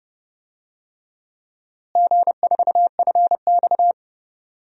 Morse code, heard as a keyed tone, about 30 wpm, 700 Hz. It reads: G4FX
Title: DXCC-100 Call Signs - 4 Characters - Repeated 3x - 30wpm